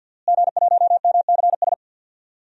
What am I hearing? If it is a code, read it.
Morse code, heard as a keyed tone, about 52 wpm, 700 Hz. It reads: G1MCS
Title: DXCC-100 Call Signs - 5 Characters - Repeated 3x - 50wpm